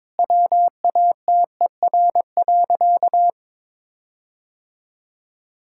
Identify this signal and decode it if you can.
Morse code, heard as a keyed tone, about 22 wpm, 700 Hz. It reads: WATER.
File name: practice / 22wpm / Sentences with 4-Word Phrases - 22wpm